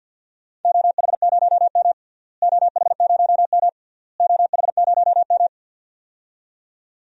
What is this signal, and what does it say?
Morse code, heard as a keyed tone, about 50 wpm, 700 Hz. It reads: OH0M OH0M OH0M
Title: DXCC-100 Call Signs - 4 Characters - Repeated 3x - 50wpm